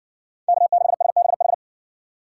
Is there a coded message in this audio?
Yes: Morse code from 0.5 to 1.5 s, heard as a keyed tone, about 58 wpm, 700 Hz. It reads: K6SBH